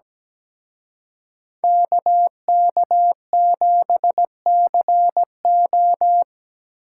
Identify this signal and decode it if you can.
Morse code, heard as a keyed tone, about 20 wpm, 700 Hz. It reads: KK7CO
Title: Speed Racing - USA 2x2 Call Signs 17wpm